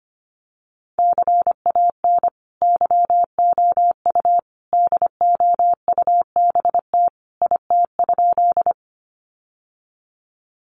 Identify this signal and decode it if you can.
Morse code, heard as a keyed tone, about 25 wpm, 700 Hz. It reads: CAN YOU DOUBT IT?
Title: Sentences from Top 1500 Words 25wpm